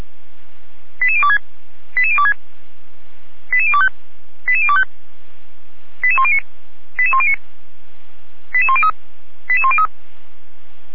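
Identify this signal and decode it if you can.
Five-tone selective call (ZVEI), heard as quick runs of five tones, tones of about 70 ms. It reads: 88026 88026 88025 88025 88109 88109 88103 88103
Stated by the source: ZVEI five-tone selective call 88102 received via SDR, from the feuerwehr-orlamuende fire brigade audio server.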